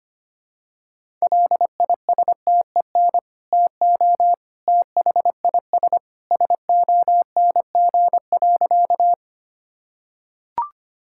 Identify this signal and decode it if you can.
Morse code, heard as a keyed tone, about 25 wpm, 700 Hz. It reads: LISTEN TO THIS SONG.
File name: practice / 25wpm / Sentences with 2-Word Phrases - 25wpm